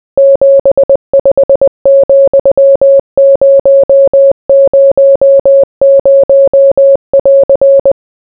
Morse code, heard as a keyed tone, about 20 wpm, 550 Hz. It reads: 75,000<AR>